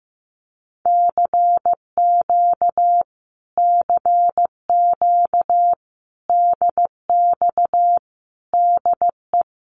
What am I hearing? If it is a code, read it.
Morse code, heard as a keyed tone, about 14 wpm, 700 Hz. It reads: CQ CQ DX DE
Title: Common QSO Phrases Encore 15wpm